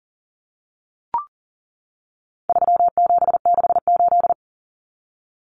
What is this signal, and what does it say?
Morse code, heard as a keyed tone, about 40 wpm, 700 Hz. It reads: 3768
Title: Numbers - 4-Digit 40wpm